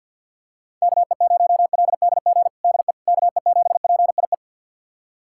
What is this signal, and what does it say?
Morse code, heard as a keyed tone, about 46 wpm, 700 Hz. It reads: KE0LDG DE KE7PIE
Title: USA Callsign DE Callsign - 6 Characters - 50wpm